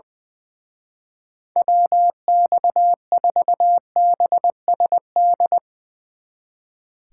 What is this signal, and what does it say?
Morse code, heard as a keyed tone, about 22 wpm, 700 Hz. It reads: WX4BSD
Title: Speed Racing - USA 2x3 Call Signs 20wpm